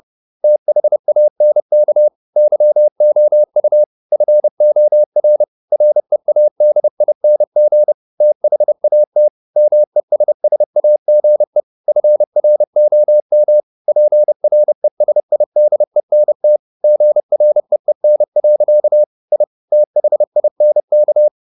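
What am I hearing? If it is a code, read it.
Morse code, heard as a keyed tone, about 30 wpm, 600 Hz. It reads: THANK YOU FOR READING THAT MESSAGE FROM PRESIDENT GREEN. I THINK
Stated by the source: Morse code at 30 wpm, the novel Never by Ken Follett in English, from CW med Gnister.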